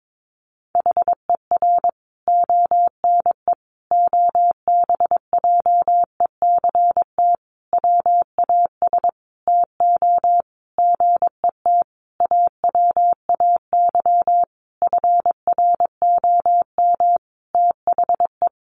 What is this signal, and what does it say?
Morse code, heard as a keyed tone, about 22 wpm, 700 Hz. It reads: HER ONE OBJECT WAS TO GET AWAY FROM THE
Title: Sentences from Top 1500 Words 22wpm